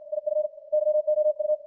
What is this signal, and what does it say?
Morse code, heard as a keyed tone, about 56 wpm, 610 Hz. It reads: IF YOU
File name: archived / CWBros